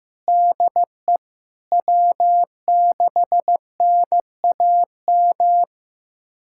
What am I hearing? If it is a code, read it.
Morse code, heard as a keyed tone, about 16 wpm, 700 Hz. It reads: DE W6NAM
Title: USA Callsign DE Callsign - 5 Characters - 15wpm